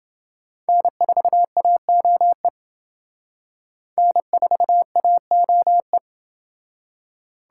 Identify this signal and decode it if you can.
Morse code, heard as a keyed tone, about 27 wpm, 700 Hz. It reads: N4AOE N4AOE